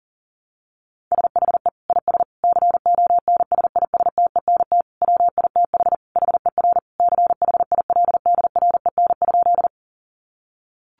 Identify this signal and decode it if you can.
Morse code, heard as a keyed tone, about 40 wpm, 700 Hz. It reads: SHE IS CONSISTENT WITH HER CHILDREN?
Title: Sentences from Top 1500 Words 40wpm